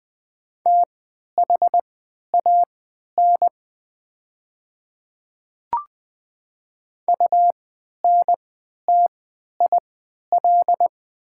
Morse code, heard as a keyed tone, about 20 wpm, spaced as for 11 wpm, 700 Hz. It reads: THAN UNTIL
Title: 3x Character Spacing - Top 300 Words - 20wpm